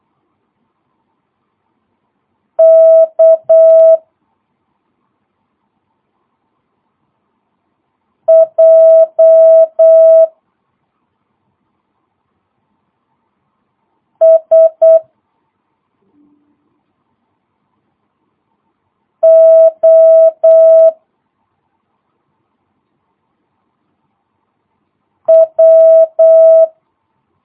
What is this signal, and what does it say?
Morse code, heard as a keyed tone, about 8 wpm, 650 Hz. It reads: K J S O W
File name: jh8cbh / 2022cw